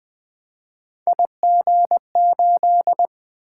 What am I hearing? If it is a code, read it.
Morse code, heard as a keyed tone, about 20 wpm, 700 Hz. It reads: IG8
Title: Three Letter-Number Combinations - 20wpm